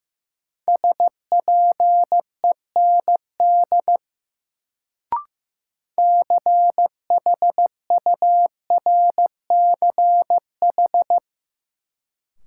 Morse code, heard as a keyed tone, about 15 wpm, 700 Hz. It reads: SPEND CHURCH
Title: Sets of 5 Words - New Word Drill - 15wpm